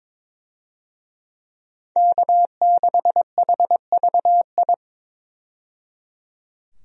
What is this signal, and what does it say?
Morse code, heard as a keyed tone, about 22 wpm, 700 Hz. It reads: K6HVI
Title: USA Call Signs - 5 Characters - 22wpm